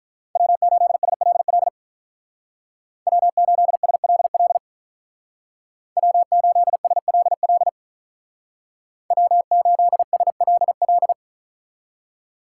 Morse code, heard as a keyed tone, about 44 wpm, 700 Hz. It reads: W8SLL W8SLL W8SLL W8SLL
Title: Speed Racing - USA 1x3 Call Signs 35wpm